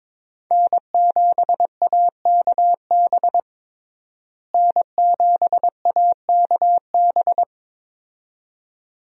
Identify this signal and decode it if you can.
Morse code, heard as a keyed tone, about 22 wpm, 700 Hz. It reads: N7AKB N7AKB
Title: USA Call Signs - 5 Characters - Repeated 2x - 22wpm